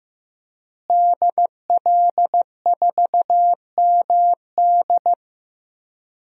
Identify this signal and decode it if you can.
Morse code, heard as a keyed tone, about 15 wpm, 700 Hz. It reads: DL4MD
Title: DXCC-100 Call Signs - 5 Characters - Repeated 3x - 15wpm